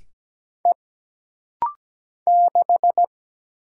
Morse code, heard as a keyed tone, about 17 wpm, 700 Hz. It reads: E 6
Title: Single Letter-Number Alt - Rapid-Fire 17wpm